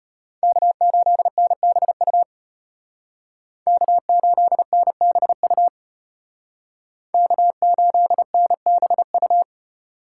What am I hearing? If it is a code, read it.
Morse code, heard as a keyed tone, about 33 wpm, 700 Hz. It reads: K8NBU K8NBU K8NBU